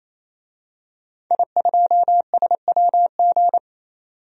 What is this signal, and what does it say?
Morse code, heard as a keyed tone, about 28 wpm, 700 Hz. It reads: I2SWG
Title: DXCC-100 Call Signs - 5 Characters - 28wpm